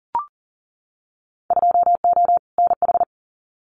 Morse code, heard as a keyed tone, about 40 wpm, 700 Hz. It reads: 2O NH
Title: Winter Field Day Exchanges 40wpm